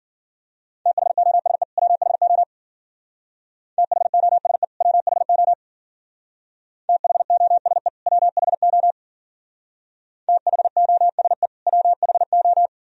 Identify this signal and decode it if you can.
Morse code, heard as a keyed tone, about 49 wpm, 700 Hz. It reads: THOSE WHO THOSE WHO THOSE WHO THOSE WHO
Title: Speed-Racing - Sets of 2 Words 40wpm